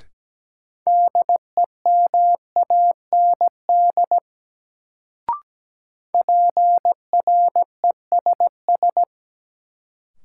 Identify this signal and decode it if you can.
Morse code, heard as a keyed tone, about 17 wpm, 700 Hz. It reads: DEMAND PRESS
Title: Sentences from Top 800 Words - New Word Drill - 17wpm